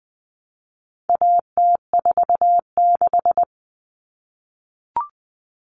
Morse code, heard as a keyed tone, about 20 wpm, 700 Hz. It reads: AT46
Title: All Cut Numbers - 4-Digit 20wpm